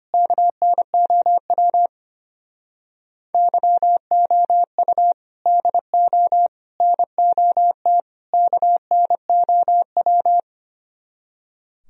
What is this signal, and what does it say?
Morse code, heard as a keyed tone, about 28 wpm, 700 Hz. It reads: KNOW YOU DO NOT KNOW
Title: Speed-Racing - Sets of 4 Words 25wpm